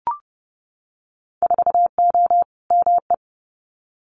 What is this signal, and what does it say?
Morse code, heard as a keyed tone, about 30 wpm, 700 Hz. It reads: 4O ME
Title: Winter Field Day Exchanges Encore 30wpm